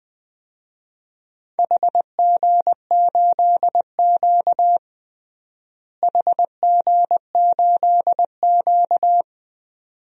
Morse code, heard as a keyed tone, about 20 wpm, 700 Hz. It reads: HG8Q HG8Q
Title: DXCC-100 Call Signs - 4 Characters - Repeated 2x - 20wpm